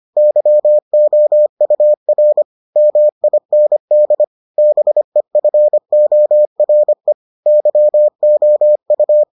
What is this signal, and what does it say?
Morse code, heard as a keyed tone, about 25 wpm, 600 Hz. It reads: YOUR MIND BEFORE YOU